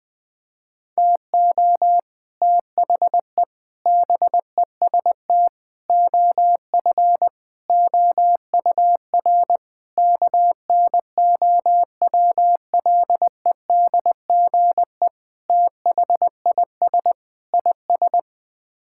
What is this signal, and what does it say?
Morse code, heard as a keyed tone, about 20 wpm, 700 Hz. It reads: TO THE BEST OF OUR KNOWLEDGE THIS IS
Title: Sets of 8 Words - 20wpm